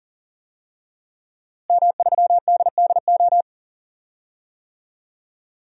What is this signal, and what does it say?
Morse code, heard as a keyed tone, about 40 wpm, 700 Hz. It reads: M3DDO
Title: DXCC-100 Call Signs - 5 Characters - Repeated 3x - 40wpm